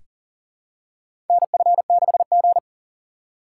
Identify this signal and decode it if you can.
Morse code, heard as a keyed tone, about 40 wpm, 700 Hz. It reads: NF6G
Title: USA Call Signs - 4 Characters - 2x Character Spacing - 40wpm